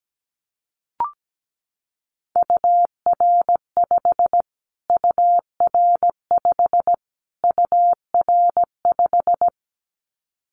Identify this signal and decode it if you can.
Morse code, heard as a keyed tone, about 17 wpm, 700 Hz. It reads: UR5 UR5 UR5
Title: Callsign Prefixes for Top 25 DXCC - Repeated 3x - 17wpm